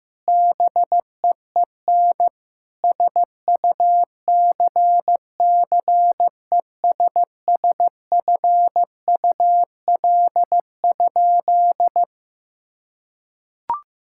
Morse code, heard as a keyed tone, about 16 wpm, 700 Hz. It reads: BEEN SUCCESSFUL?
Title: Sentences from Top 1000 Words - Review Part 6 - 15wpm